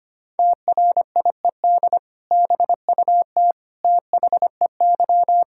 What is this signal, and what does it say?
Morse code, heard as a keyed tone, about 25 wpm, 700 Hz. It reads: TRIED BUT THEY
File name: practice / 25wpm / Sentences with 2-Word Phrases - 25wpm